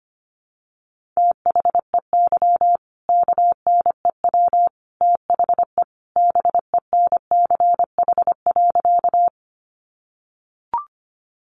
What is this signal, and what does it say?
Morse code, heard as a keyed tone, about 25 wpm, 700 Hz. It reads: THEY KNEW THE BENCH.
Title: Sentences from Top 1400 Words 25wpm